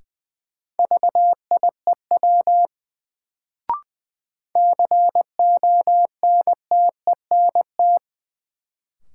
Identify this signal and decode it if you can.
Morse code, heard as a keyed tone, about 20 wpm, 700 Hz. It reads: VIEW CONTENT
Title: Sentences from Top 900 Words - New Word Drill - 20wpm